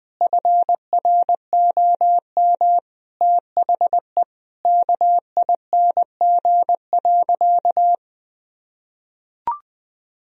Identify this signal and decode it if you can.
Morse code, heard as a keyed tone, about 20 wpm, 700 Hz. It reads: FROM THE KING.
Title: Sentences from Top 1000 Words - Review Part 2 - 20wpm